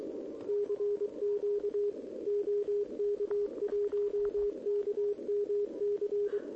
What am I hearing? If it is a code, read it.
Morse code, heard as a keyed tone, about 23 wpm, 410 Hz, with radio noise. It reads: CQ OK1KMC
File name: OK QRP 2026 / OKQRP2026